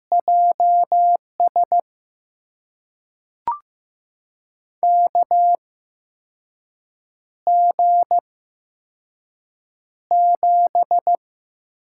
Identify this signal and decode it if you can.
Morse code, heard as a keyed tone, about 15 wpm, spaced as for 5 wpm, 700 Hz. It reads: JS KG7
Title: USA Call Signs - 6 Characters - 8x Character Spacing - 15wpm